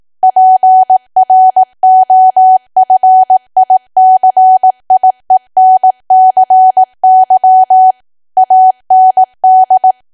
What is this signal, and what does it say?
Morse code, heard as a keyed tone, about 18 wpm, 750 Hz. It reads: PROFICIENCY AND
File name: cwa / W1AW-Level2 / 18WPM